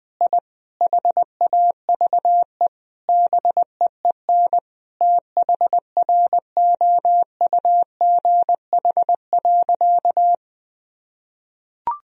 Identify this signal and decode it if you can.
Morse code, heard as a keyed tone, about 20 wpm, 700 Hz. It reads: I HAVE BEEN THROUGH.
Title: Sentences from Top 1000 Words - Review Part 1 - 20wpm